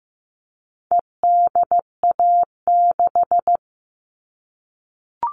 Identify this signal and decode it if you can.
Morse code, heard as a keyed tone, about 15 wpm, 700 Hz. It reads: EDA6
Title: All Cut Numbers - 4-Digit 15wpm